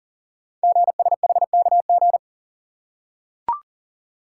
Morse code, heard as a keyed tone, about 39 wpm, 690 Hz. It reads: GSHKG
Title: Code Groups - 5 Letters - 40wpm